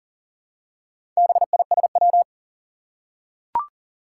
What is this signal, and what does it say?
Morse code, heard as a keyed tone, about 40 wpm, 700 Hz. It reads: BISW
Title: Four Challenging Letter-Number Combinations 40wpm